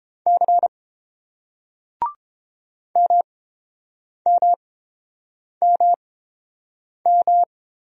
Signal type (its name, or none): Morse code